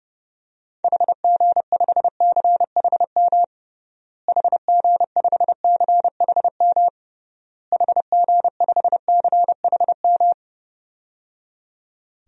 Morse code, heard as a keyed tone, about 30 wpm, 700 Hz. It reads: HG5CHM HG5CHM HG5CHM
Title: DXCC-100 Call Signs - 6 Characters - Repeated 3x - 30wpm